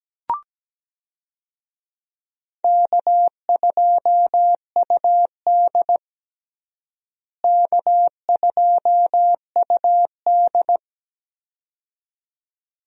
Morse code, heard as a keyed tone, about 17 wpm, 700 Hz. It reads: K2UD K2UD
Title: USA Call Signs - 4 Characters - Repeated 2x - 17wpm